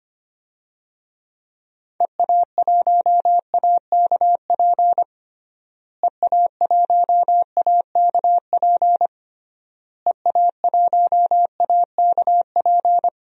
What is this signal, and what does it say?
Morse code, heard as a keyed tone, about 25 wpm, 700 Hz. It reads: EA1AKP EA1AKP EA1AKP
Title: DXCC-100 Call Signs - 6 Characters - Repeated 3x - 25wpm